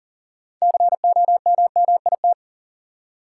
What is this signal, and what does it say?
Morse code, heard as a keyed tone, about 40 wpm, 700 Hz. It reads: COMMIT
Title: Six Letter Words - 40wpm